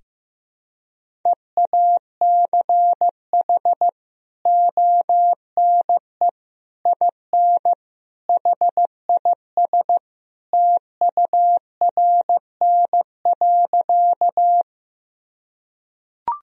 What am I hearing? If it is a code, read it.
Morse code, heard as a keyed tone, about 15 wpm, 700 Hz. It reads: EACH ONE IN HIS TURN.
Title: Sentences from Top 1000 Words - Review Part 1 - 15wpm